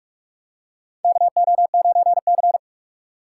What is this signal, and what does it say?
Morse code, heard as a keyed tone, about 46 wpm, 700 Hz. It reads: KO9C
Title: USA Call Signs - 4x Character Spacing - 45wpm